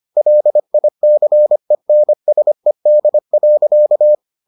Morse code, heard as a keyed tone, about 25 wpm, 600 Hz. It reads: LICENSED.